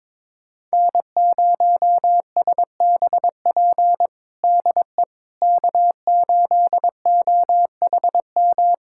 Morse code, heard as a keyed tone, about 22 wpm, 700 Hz. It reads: N0SBP DE K8OHM
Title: USA Callsign DE Callsign - 5 Characters - 22wpm